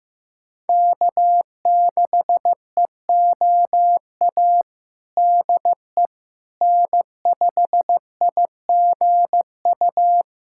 Morse code, heard as a keyed tone, about 16 wpm, 700 Hz. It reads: K6EOA DE N5IGU
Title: USA Callsign DE Callsign - 5 Characters - 15wpm